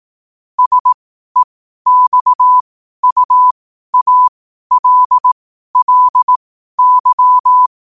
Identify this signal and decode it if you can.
Morse code, heard as a keyed tone, about 18 wpm, spaced as for 12 wpm, 1000 Hz. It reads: SEXUALLY
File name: qotdfiles / 12wpm